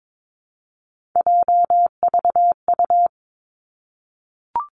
Morse code, heard as a keyed tone, about 22 wpm, 700 Hz. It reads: JVU
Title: Three Challenging Letter-Number Combinations 22wpm